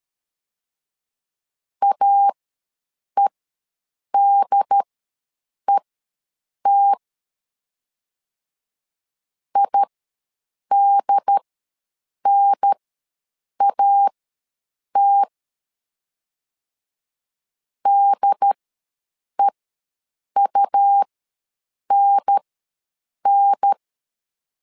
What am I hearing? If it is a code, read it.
Morse code, heard as a keyed tone, about 13 wpm, spaced as for 7 wpm, 800 Hz. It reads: AEDET IDNAT DEUNN
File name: LEZIONI / 2 UHDN / 2 UHDN ws 12wpm 1708